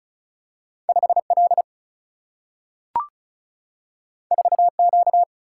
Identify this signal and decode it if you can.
Morse code, heard as a keyed tone, about 36 wpm, 700 Hz. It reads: 5 L 4 Q
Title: Single-Letter Single-Number 35wpm